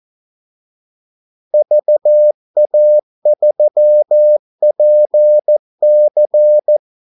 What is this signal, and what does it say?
Morse code, heard as a keyed tone, about 14 wpm, 600 Hz. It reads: VA3PC
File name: Callsigns-14wpm